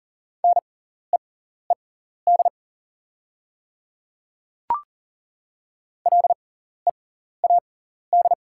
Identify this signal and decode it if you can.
Morse code, heard as a keyed tone, about 40 wpm, spaced as for 14 wpm, 690 Hz. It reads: NEED LEAD
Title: 6x Character Spacing - Top 500 Words - 40wpm